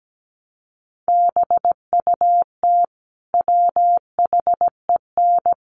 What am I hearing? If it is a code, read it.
Morse code, heard as a keyed tone, about 17 wpm, 700 Hz. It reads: BUT WHEN
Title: Sentences from Top 100 Words 17wpm